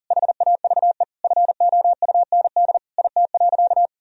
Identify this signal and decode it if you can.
Morse code, heard as a keyed tone, about 37 wpm, 700 Hz. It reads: HAVE FOUND IT.
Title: Sentences from Top 1000 Words - Review Part 1 - 40wpm